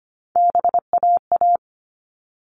Morse code, heard as a keyed tone, about 25 wpm, 700 Hz. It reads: BAA
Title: All Cut Numbers - 3-Digit 25wpm